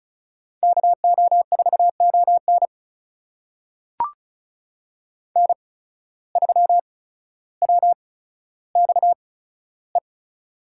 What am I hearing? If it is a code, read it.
Morse code, heard as a keyed tone, about 36 wpm, spaced as for 15 wpm, 700 Hz. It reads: KO4ON N3WXE
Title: USA Call Signs - 8x Character Spacing - 35wpm